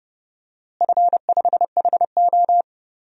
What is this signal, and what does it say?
Morse code, heard as a keyed tone, about 30 wpm, 700 Hz. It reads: F5HO
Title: DXCC-100 Call Signs - 4 Characters - Repeated 3x - 30wpm